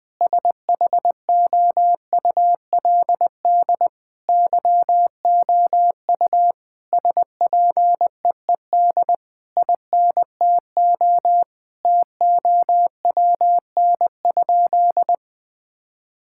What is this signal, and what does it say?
Morse code, heard as a keyed tone, about 20 wpm, 700 Hz. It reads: SHOULD YOU SPEED INTO TOWN?
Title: Sentences from Top 1000 Words - Review Part 5 - 20wpm